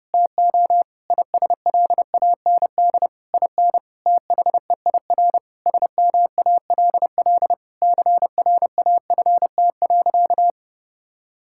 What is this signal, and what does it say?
Morse code, heard as a keyed tone, about 32 wpm, 700 Hz. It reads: TO ISLAND IN THEIR SMALL CRAFT.
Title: Sentences from Top 1000 Words - Review Part 6 - 30wpm